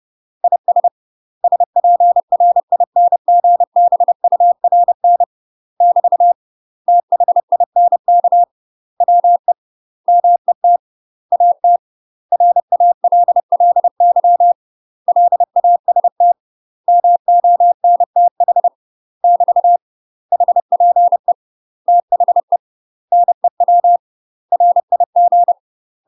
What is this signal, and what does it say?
Morse code, heard as a keyed tone, about 30 wpm, 700 Hz. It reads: IS SPRINGBURN <BT> THINK WE MET AT RALLY LAST MONTH <BT> HPE THE NEW RIG
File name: UK MASTER QSOs TRAINING / 30WPM / UK MASTER QSOs 0045